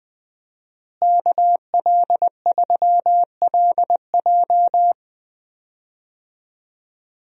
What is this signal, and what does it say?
Morse code, heard as a keyed tone, about 20 wpm, 700 Hz. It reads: KL3LJ
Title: USA Call Signs - 5 Characters - Repeated 2x - 20wpm